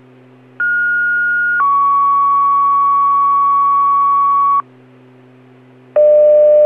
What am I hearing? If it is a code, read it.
Two-tone page, heard as a one-second tone, then a three-second tone, about 1430 then 1120 Hz. It is heard from 0.6 to 4.6 s.